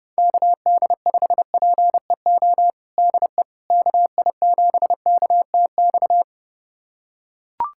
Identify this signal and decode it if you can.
Morse code, heard as a keyed tone, about 30 wpm, 700 Hz. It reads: KD5PEO DE KI7KTX
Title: USA Callsign DE Callsign - 6 Characters - 30wpm